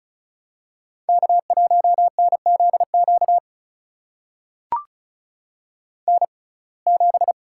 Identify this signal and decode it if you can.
Morse code, heard as a keyed tone, about 36 wpm, spaced as for 25 wpm, 700 Hz. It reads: K1NZQ N7
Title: USA Call Signs - 5 Characters - 6x Character Spacing - 35wpm